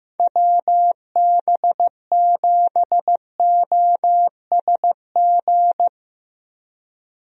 Morse code, heard as a keyed tone, about 15 wpm, 700 Hz. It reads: WB7OSG
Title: USA Call Signs - 6 Characters - Repeated 3x - 15wpm